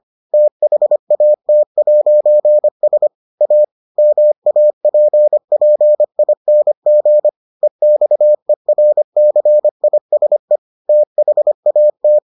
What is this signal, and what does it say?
Morse code, heard as a keyed tone, about 25 wpm, 600 Hz. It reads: THAT'S A MAPPING EXERCISE THAT